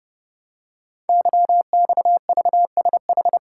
Morse code, heard as a keyed tone, about 28 wpm, 700 Hz. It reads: YXVSH
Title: Code Groups - 5 Letter:Numbers - 30wpm